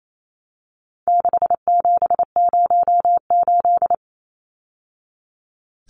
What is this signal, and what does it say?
Morse code, heard as a keyed tone, about 28 wpm, 700 Hz. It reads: 6708
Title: Numbers - 4-Digit 28wpm